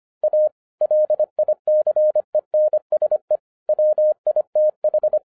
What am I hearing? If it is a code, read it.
Morse code, heard as a keyed tone, about 25 wpm, 610 Hz. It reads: A LICENSE WITH
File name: 2011-10-mp3 / n04-JUST MOVED 25wpm